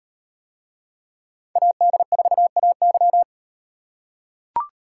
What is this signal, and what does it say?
Morse code, heard as a keyed tone, about 38 wpm, 700 Hz. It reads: AD4AY